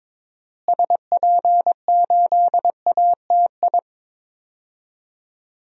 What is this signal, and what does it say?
Morse code, heard as a keyed tone, about 22 wpm, 700 Hz. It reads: SP8ATI
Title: DXCC-100 Call Signs - 6 Characters - Repeated 3x - 22wpm